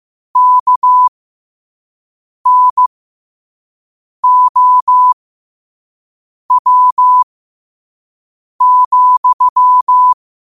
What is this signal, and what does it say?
Morse code, heard as a keyed tone, about 15 wpm, spaced as for 5 wpm, 1000 Hz. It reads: KNOW,
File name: qotdfiles / 5wpm